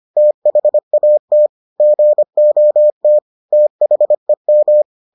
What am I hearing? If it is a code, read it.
Morse code, heard as a keyed tone, about 25 wpm, 600 Hz. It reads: THAT GOT THEM